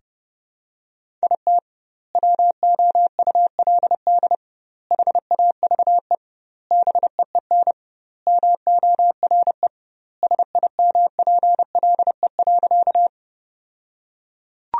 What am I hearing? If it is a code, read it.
Morse code, heard as a keyed tone, about 30 wpm, 700 Hz. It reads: IT WOULD HAVE BEEN MORE SIMPLE.
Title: 2x Word-Spacing - Sentences from Top 400 Words - 30wpm